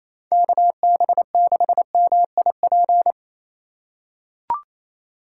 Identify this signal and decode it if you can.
Morse code, heard as a keyed tone, about 28 wpm, 700 Hz. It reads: KB6MIP
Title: USA Call Signs - 2x Character Spacing - 28wpm